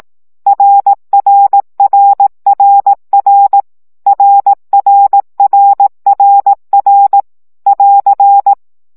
Morse code, heard as a keyed tone, about 18 wpm, 800 Hz. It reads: RRRRR RRRRR <AR>
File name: CW gyakorlas r 15 baud